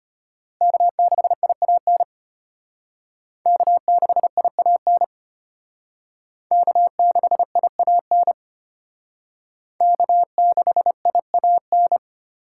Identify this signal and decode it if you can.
Morse code, heard as a keyed tone, about 32 wpm, 700 Hz. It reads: K6IAN K6IAN K6IAN K6IAN